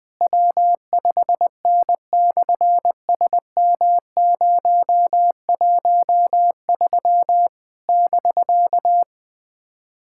Morse code, heard as a keyed tone, about 20 wpm, 700 Hz. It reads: W5N/SM013 <BK>
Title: SOTA QSO Practice 20wpm